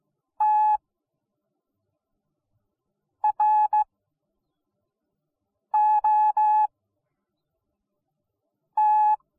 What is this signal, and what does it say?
Morse code, heard as a keyed tone, about 17 wpm, spaced as for 4 wpm, 850 Hz. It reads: TROT